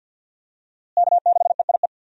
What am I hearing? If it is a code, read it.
Morse code, heard as a keyed tone, about 46 wpm, 700 Hz. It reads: K6EIE
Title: USA Call Signs - 5 Characters - Repeated 3x - 50wpm